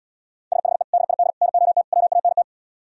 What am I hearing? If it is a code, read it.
Morse code, heard as a keyed tone, about 77 wpm, 690 Hz. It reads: SHE DID NOT WANT